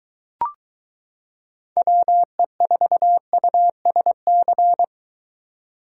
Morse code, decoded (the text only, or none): WE4USC